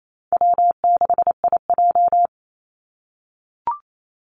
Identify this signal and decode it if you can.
Morse code, heard as a keyed tone, about 28 wpm, 700 Hz. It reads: W6IJ
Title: Call Signs - 28wpm